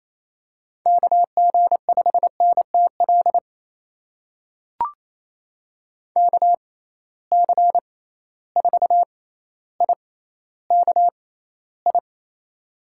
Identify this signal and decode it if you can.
Morse code, heard as a keyed tone, about 28 wpm, spaced as for 14 wpm, 700 Hz. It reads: KG5NTL KC4IKI
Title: USA Call Signs - 6 Characters - 6x Character Spacing - 28wpm